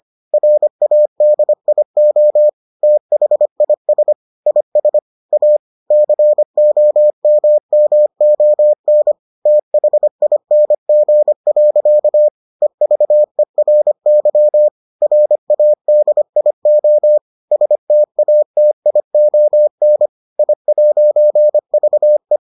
Morse code, heard as a keyed tone, about 25 wpm, 600 Hz. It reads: RADIO THIS IS A COMMON THING. EVERY RADIO STATION I'VE